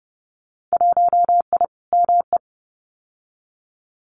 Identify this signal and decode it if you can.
Morse code, heard as a keyed tone, about 30 wpm, 700 Hz. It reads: 1I ME
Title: Winter Field Day Exchanges 30wpm